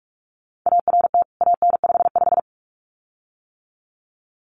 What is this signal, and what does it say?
Morse code, heard as a keyed tone, about 44 wpm, 700 Hz. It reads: ART AN55
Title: CWT Exchange 45wpm